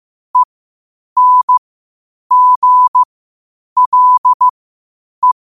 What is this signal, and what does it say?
Morse code, heard as a keyed tone, about 15 wpm, spaced as for 8 wpm, 1000 Hz. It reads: ENGLE
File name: qotdfiles / 8wpm